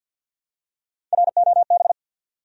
Morse code, heard as a keyed tone, about 52 wpm, 700 Hz. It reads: ROB
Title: USA Common Names - Max 3 Letters - 50wpm